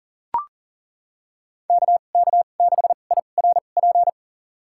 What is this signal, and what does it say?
Morse code, heard as a keyed tone, about 40 wpm, spaced as for 29 wpm, 700 Hz. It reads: KK6IRP
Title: USA Call Signs - 6 Characters - 2x Character Spacing - 40wpm